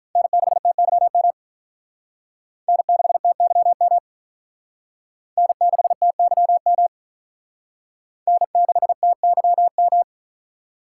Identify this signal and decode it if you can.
Morse code, heard as a keyed tone, about 45 wpm, 700 Hz. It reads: N6TYM N6TYM N6TYM N6TYM